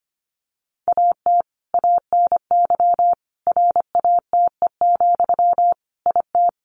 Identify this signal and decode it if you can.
Morse code, heard as a keyed tone, about 25 wpm, 700 Hz. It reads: AT ANY RATE, IT
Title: Sentences from Top 1200 Words 25wpm